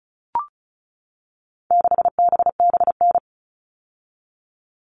Morse code, heard as a keyed tone, about 34 wpm, 700 Hz. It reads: 6BBN
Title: All Cut Numbers - 4-Digit 35wpm